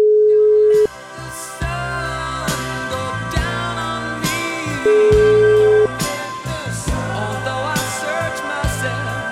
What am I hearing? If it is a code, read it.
Ringback tone (Europe, 425 Hz), with noise.